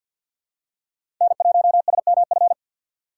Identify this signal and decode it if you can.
Morse code, heard as a keyed tone, about 49 wpm, 700 Hz. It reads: N1SMF
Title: USA Call Signs - 5 Characters - 6x Character Spacing - 50wpm